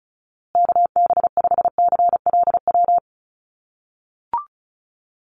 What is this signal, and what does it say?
Morse code, heard as a keyed tone, about 34 wpm, 700 Hz. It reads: KB5CLW
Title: USA Call Signs Repeated 2x 35wpm